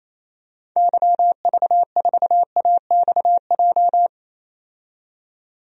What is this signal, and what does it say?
Morse code, heard as a keyed tone, about 28 wpm, 700 Hz. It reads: YV4AXJ
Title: DXCC-100 Call Signs - 6 Characters - 28wpm